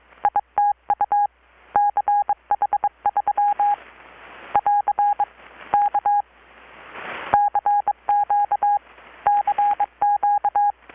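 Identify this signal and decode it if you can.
Morse code, heard as a keyed tone, about 24 wpm, 820 Hz. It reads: ITU CH3 <AR> K CQ CQ
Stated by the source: Morse from KPH's Night of Nights XX maritime radiotelegraphy event, received in Chiapas, Mexico about 3,800 km away on a Kenwood R-600 with a grounded 10 m long-wire antenna.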